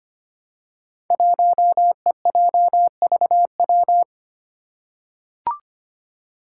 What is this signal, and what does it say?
Morse code, heard as a keyed tone, about 25 wpm, 700 Hz. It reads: 1EJVW
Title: Five Challenging Letter-Number Combinations 25wpm